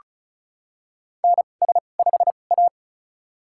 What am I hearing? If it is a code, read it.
Morse code, heard as a keyed tone, about 36 wpm, spaced as for 25 wpm, 700 Hz. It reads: NS5A
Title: USA Call Signs - 4 Characters - 2x Character Spacing - 35wpm